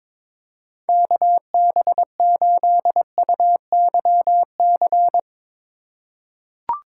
Morse code, heard as a keyed tone, about 22 wpm, 700 Hz. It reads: KB8UYC
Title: USA Call Signs - 6 Characters - 3x Character Spacing - 22wpm